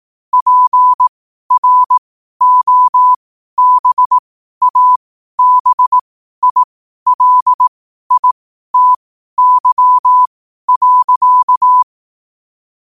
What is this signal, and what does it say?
Morse code, heard as a keyed tone, about 18 wpm, spaced as for 12 wpm, 1000 Hz. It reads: PROBABILITY.
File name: qotdfiles / 12wpm